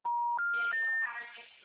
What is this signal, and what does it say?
Special information tone, with noise.
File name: out- -2008- - - .52809